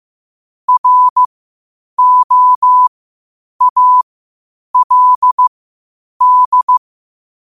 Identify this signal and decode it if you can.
Morse code, heard as a keyed tone, about 15 wpm, spaced as for 8 wpm, 1000 Hz. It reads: ROALD
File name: qotdfiles / 8wpm